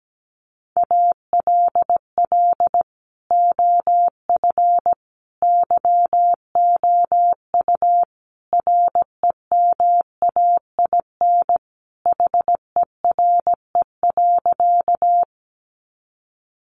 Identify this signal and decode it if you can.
Morse code, heard as a keyed tone, about 17 wpm, 700 Hz. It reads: ALL OF YOU REMAIN HERE.
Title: Sentences from Top 1200 Words 17wpm